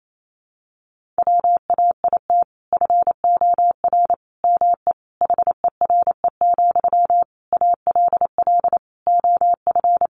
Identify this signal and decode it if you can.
Morse code, heard as a keyed tone, about 28 wpm, 700 Hz. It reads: WAIT FOR ME HERE, ALL OF
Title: Sentences from Top 500 Words 28wpm